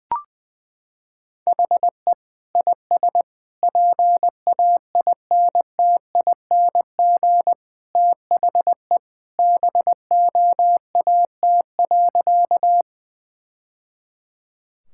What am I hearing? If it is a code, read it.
Morse code, heard as a keyed tone, about 21 wpm, 700 Hz. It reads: HE IS PAINTING THE BOAT.
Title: Sentences from Top 1000 Words - Review Part 6 - 20wpm